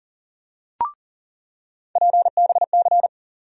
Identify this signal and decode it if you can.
Morse code, heard as a keyed tone, about 40 wpm, 700 Hz. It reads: PBC
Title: USA Call Sign Postfixes - 3 Letters 40wpm